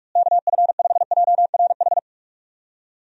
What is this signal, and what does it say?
Morse code, heard as a keyed tone, about 45 wpm, 700 Hz. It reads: KF5JRH
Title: USA Call Signs - 6 Characters - Repeated 2x - 45wpm